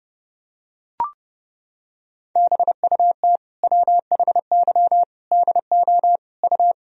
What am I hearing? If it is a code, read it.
Morse code, heard as a keyed tone, about 32 wpm, 700 Hz. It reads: BUT WHY DO U
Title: Sentences from Top 200 Words and Abbreviations 30wpm